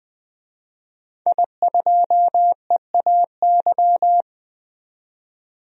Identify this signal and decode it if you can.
Morse code, heard as a keyed tone, about 21 wpm, 700 Hz. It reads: I2EAY
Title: DXCC-100 Call Signs - 5 Characters - 20wpm